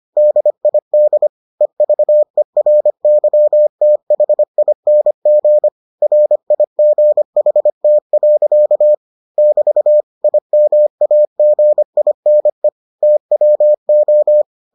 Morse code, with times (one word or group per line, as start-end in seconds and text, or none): 0.2-1.3 DID
1.6-5.7 EVERYTHING
6.0-8.9 RIGHT.
9.4-10.0 <BT>
10.2-12.7 IMAGINE
13.0-14.4 TWO